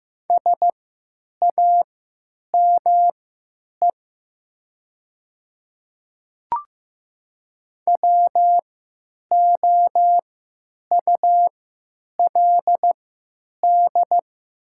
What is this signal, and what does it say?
Morse code, heard as a keyed tone, about 15 wpm, spaced as for 9 wpm, 700 Hz. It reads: SAME WOULD
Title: 3x Character Spacing - Top 200 Words - 15wpm